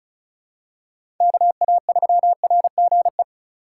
Morse code, heard as a keyed tone, about 34 wpm, 700 Hz. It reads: KA3RGE
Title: USA Call Signs - 4x Character Spacing - 35wpm